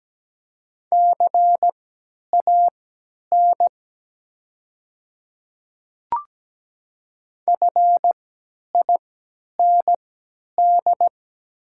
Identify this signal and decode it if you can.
Morse code, heard as a keyed tone, about 17 wpm, spaced as for 10 wpm, 690 Hz. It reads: CAN FIND
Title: 3x Character Spacing - Top 200 Words - 17wpm